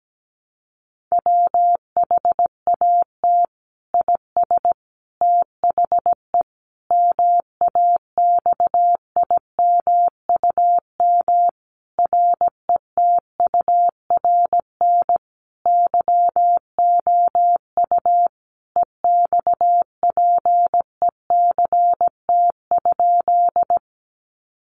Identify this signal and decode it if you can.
Morse code, heard as a keyed tone, about 17 wpm, 700 Hz. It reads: WHAT IS THE MAXIMUM RETURN YOU EXPECT?
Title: Sentences from Top 1300 Words 17wpm